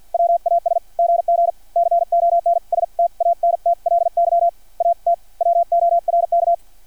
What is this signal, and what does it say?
Morse code, heard as a keyed tone, about 50 wpm, spaced as for 43 wpm, 680 Hz. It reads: WAS GG CONSTANTLY AT WORK
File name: archived / CRSnet